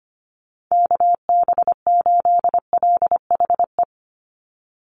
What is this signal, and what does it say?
Morse code, heard as a keyed tone, about 25 wpm, 700 Hz. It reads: KB8LHE
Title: USA Call Signs 25wpm